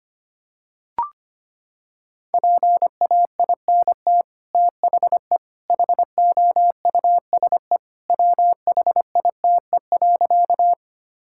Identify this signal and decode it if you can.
Morse code, heard as a keyed tone, about 25 wpm, 700 Hz. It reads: PAINT THE HOUSE WHITE.
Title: Sentences from Top 1000 Words - Review Part 6 - 25wpm